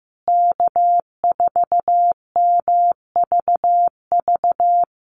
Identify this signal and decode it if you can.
Morse code, heard as a keyed tone, about 15 wpm, 700 Hz. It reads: K4MVV
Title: USA Call Signs Repeated 3x 15wpm